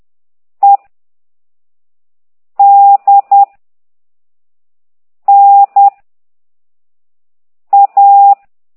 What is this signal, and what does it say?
Morse code, heard as a keyed tone, about 10 wpm, 800 Hz. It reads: E D N A
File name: ct2idw / exerc9